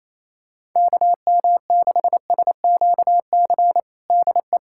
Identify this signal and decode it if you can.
Morse code, heard as a keyed tone, about 28 wpm, 700 Hz. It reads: KM6SQC DE
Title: USA Callsign DE Callsign - 6 Characters - 28wpm